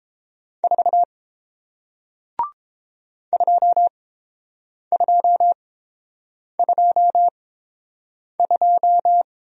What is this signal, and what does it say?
Morse code, heard as a keyed tone, about 30 wpm, spaced as for 7 wpm, 700 Hz. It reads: 4 2222